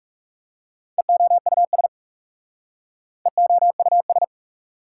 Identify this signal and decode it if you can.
Morse code, heard as a keyed tone, about 37 wpm, 690 Hz. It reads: EOUS EOUS
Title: Speed-Racing Prefixes and Suffixes 30wpm